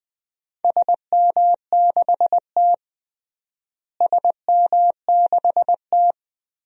Morse code, heard as a keyed tone, about 20 wpm, 700 Hz. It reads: SM6T SM6T
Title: DXCC-100 Call Signs - 4 Characters - Repeated 3x - 20wpm